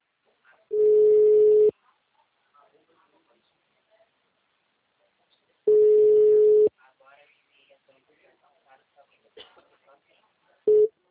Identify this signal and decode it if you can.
Ringback tone (Europe, 425 Hz), with speech.
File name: out- -2009- - - .3149